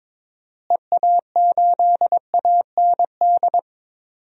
Morse code, heard as a keyed tone, about 21 wpm, 700 Hz. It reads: EA8AND
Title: DXCC-100 Call Signs - 6 Characters - Repeated 2x - 22wpm